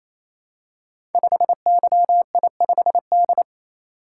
Morse code, heard as a keyed tone, about 28 wpm, 700 Hz. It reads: 5YI5D
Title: Code Groups - 5 Letter:Numbers - 28wpm